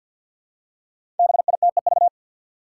Morse code, heard as a keyed tone, about 52 wpm, 700 Hz. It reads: BITEV
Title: Five Challenging Letter-Number Combinations 50wpm